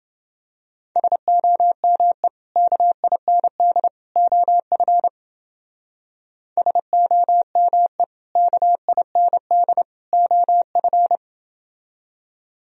Morse code, heard as a keyed tone, about 29 wpm, 700 Hz. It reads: SOME KIND OF SOME KIND OF